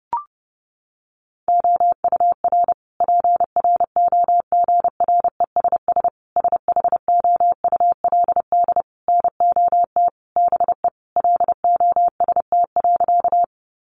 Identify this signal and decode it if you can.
Morse code, heard as a keyed tone, about 30 wpm, 700 Hz. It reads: OUR PROGRESS SHOULD NOT BE LOST.
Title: Sentences from Top 1400 Words 30wpm